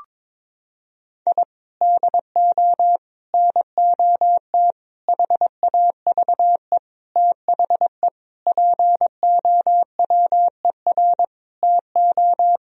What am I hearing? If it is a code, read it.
Morse code, heard as a keyed tone, about 22 wpm, 700 Hz. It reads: I DO NOT HAVE THE POWER TO
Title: Sentences from Top 1000 Words - Review Part 3 - 22wpm